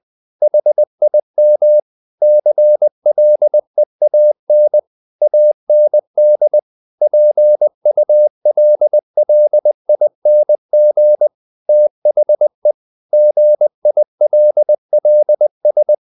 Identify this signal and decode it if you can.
Morse code, heard as a keyed tone, about 20 wpm, 600 Hz. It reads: HIM CLEAN AND PULLING THE GILLS
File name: the old man and the sea part 2 20wpm